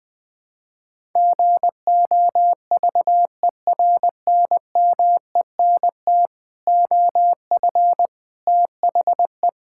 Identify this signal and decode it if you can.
Morse code, heard as a keyed tone, about 20 wpm, 700 Hz. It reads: GOVERNMENT OF THE
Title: Sentences with 9-Word Phrases - 20wpm